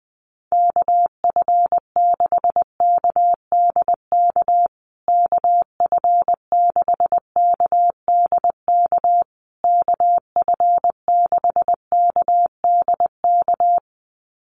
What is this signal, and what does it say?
Morse code, heard as a keyed tone, about 20 wpm, 700 Hz. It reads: KF6KDK KF6KDK KF6KDK
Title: USA Call Signs Repeated 3x 20wpm